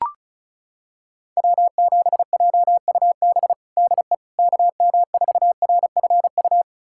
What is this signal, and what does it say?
Morse code, heard as a keyed tone, about 36 wpm, 700 Hz. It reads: W7JUB DE KM4RFU
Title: USA Callsign DE Callsign 35wpm